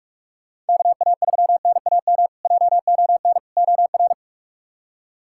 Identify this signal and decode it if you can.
Morse code, heard as a keyed tone, about 44 wpm, 700 Hz. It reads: KA3NAM JON OR
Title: K1USN SST QSO Practice - 45wpm